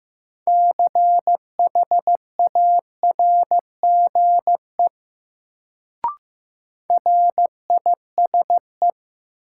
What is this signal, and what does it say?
Morse code, heard as a keyed tone, about 15 wpm, 700 Hz. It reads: CHARGE RISE
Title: Sentences from Top 900 Words - New Word Drill - 15wpm